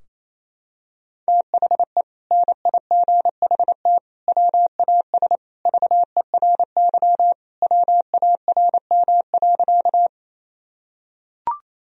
Morse code, heard as a keyed tone, about 28 wpm, 700 Hz. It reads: THE NIGHT WAS VERY WARM.
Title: Sentences from Top 1000 Words - Review Part 2 - 28wpm